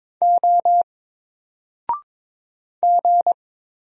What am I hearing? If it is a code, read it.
Morse code, heard as a keyed tone, about 22 wpm, 700 Hz. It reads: O G
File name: Single-Letter-Number-22wpm